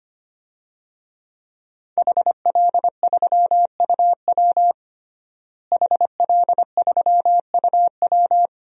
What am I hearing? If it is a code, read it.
Morse code, heard as a keyed tone, about 25 wpm, 700 Hz. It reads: HL3UW HL3UW
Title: DXCC-100 Call Signs - 5 Characters - Repeated 2x - 25wpm